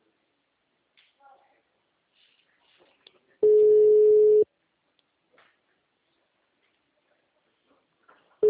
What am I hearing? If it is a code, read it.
Ringback tone (Europe, 425 Hz), with speech.